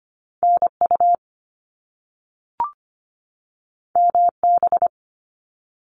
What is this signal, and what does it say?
Morse code, heard as a keyed tone, about 25 wpm, 700 Hz. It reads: NU MB
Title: Canadian Provinces and Territories 25wpm